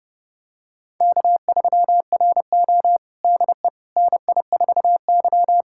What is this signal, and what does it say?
Morse code, heard as a keyed tone, about 32 wpm, 700 Hz. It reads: K3RO DE NI4Y
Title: USA Callsign DE Callsign - 4 Characters - 30wpm